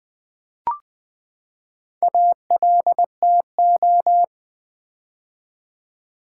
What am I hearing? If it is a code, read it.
Morse code, heard as a keyed tone, about 20 wpm, 700 Hz. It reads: ALTO
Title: Build A Word - 4 Letters At A Time - 10-Letter Words - 20wpm